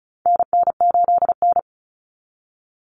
Morse code, heard as a keyed tone, about 34 wpm, 700 Hz. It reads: NN8N
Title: Common Cut Numbers - 4-Digit 35wpm